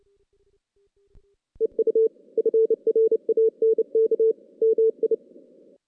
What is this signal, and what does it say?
Morse code, heard as a keyed tone, about 31 wpm, 450 Hz. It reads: EU FRANK MI
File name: 2016 JAN CW NAQP / 2016 Jan naqp k5ot - 0-5